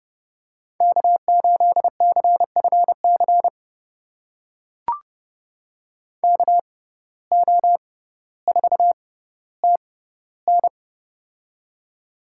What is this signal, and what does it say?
Morse code, heard as a keyed tone, about 30 wpm, spaced as for 15 wpm, 700 Hz. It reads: K8CFC KO4TN
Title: USA Call Signs - 5 Characters - 6x Character Spacing - 30wpm